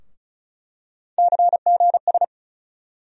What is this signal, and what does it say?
Morse code, heard as a keyed tone, about 34 wpm, 700 Hz. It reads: CGS
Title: USA Call Sign Postfixes - 3 Letters 35wpm